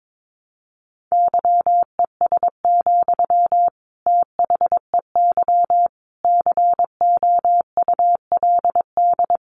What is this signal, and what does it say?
Morse code, heard as a keyed tone, about 22 wpm, 700 Hz. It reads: YES, THEY COULD
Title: Sentences from Top 700 Words 22wpm